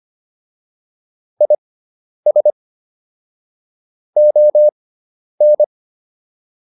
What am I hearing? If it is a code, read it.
Morse code, heard as a keyed tone, about 25 wpm, spaced as for 10 wpm, 600 Hz. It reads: IS ON